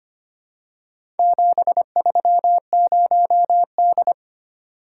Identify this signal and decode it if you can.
Morse code, heard as a keyed tone, about 25 wpm, 700 Hz. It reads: 730D
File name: practice / 25wpm / Four Letter-Number Combinations - 25wpm